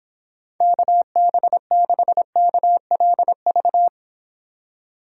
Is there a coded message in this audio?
Yes: Morse code from 0.6 to 3.9 s, heard as a keyed tone, about 24 wpm, 700 Hz. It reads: KB6KLV